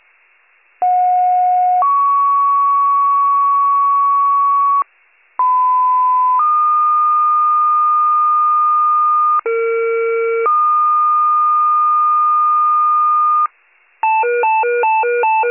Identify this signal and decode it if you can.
Two-tone page, heard as a one-second tone, then a three-second tone, about 730 then 1120 Hz, about 1010 then 1220 Hz, about 460 then 1220 Hz. It is heard from 0.8 to 13.5 s.